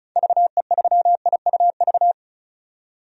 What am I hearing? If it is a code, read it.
Morse code, heard as a keyed tone, about 36 wpm, 700 Hz. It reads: VE3IUV
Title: DXCC-100 Call Signs - 6 Characters - 35wpm